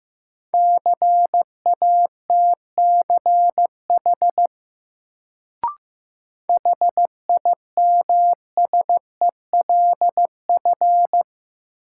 Morse code, heard as a keyed tone, about 15 wpm, 700 Hz. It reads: CATCH HIMSELF
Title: Sentences from Top 1500 Words - New Word Drill - 15wpm